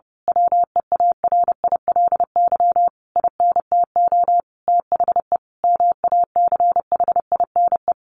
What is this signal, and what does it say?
Morse code, heard as a keyed tone, about 30 wpm, 700 Hz. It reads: WEARILY INTO THE MACHINE
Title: A Sound Of Thunder 30wpm